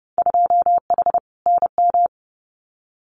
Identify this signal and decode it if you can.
Morse code, heard as a keyed tone, about 31 wpm, 700 Hz. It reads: 2H NM
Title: Winter Field Day Exchanges Encore 30wpm